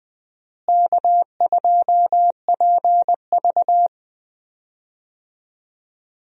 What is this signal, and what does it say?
Morse code, heard as a keyed tone, about 20 wpm, 700 Hz. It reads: K2PV
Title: USA Call Signs - 4 Characters - Repeated 3x - 20wpm